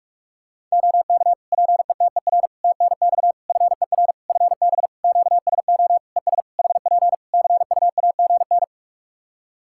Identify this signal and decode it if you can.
Morse code, heard as a keyed tone, about 44 wpm, 700 Hz. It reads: OK PETER TNX FER FB QSO ES HP CUAGN
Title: K1USN SST QSO Practice 45wpm